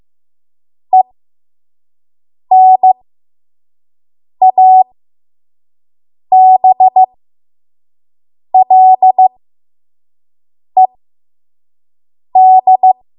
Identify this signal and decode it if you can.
Morse code, heard as a keyed tone, about 15 wpm, spaced as for 5 wpm, 750 Hz. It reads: ENABLED
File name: Morse / 05WPM